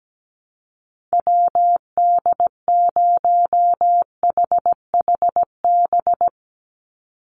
Morse code, heard as a keyed tone, about 17 wpm, 700 Hz. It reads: WD0HHB
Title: USA Call Signs 17wpm